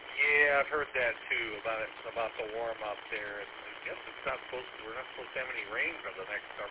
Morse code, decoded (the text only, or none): none